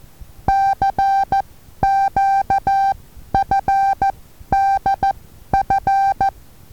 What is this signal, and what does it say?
Morse code, heard as a keyed tone, about 14 wpm, spaced as for 11 wpm, 780 Hz. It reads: CQFDF
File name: Taviro HA6VA B